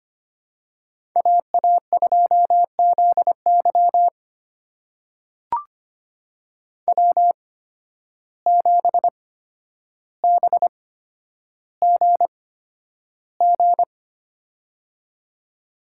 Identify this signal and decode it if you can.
Morse code, heard as a keyed tone, about 25 wpm, spaced as for 11 wpm, 700 Hz. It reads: AA2ZY W7BGG
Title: USA Call Signs - 5 Characters - 8x Character Spacing - 25wpm